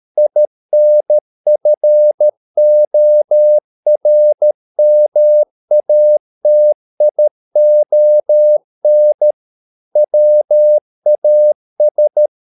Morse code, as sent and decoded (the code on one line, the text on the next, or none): .. -. ..-. --- .-. -- .- - .. --- -. / .-- .- ...
INFORMATION WAS